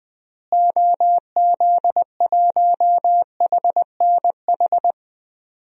Morse code, heard as a keyed tone, about 21 wpm, 700 Hz. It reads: OZ1HNH
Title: DXCC-100 Call Signs - 6 Characters - Repeated 3x - 20wpm